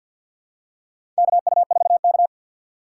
Morse code, heard as a keyed tone, about 54 wpm, 700 Hz. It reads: KU4K
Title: USA Call Signs - 6x Character Spacing - 50wpm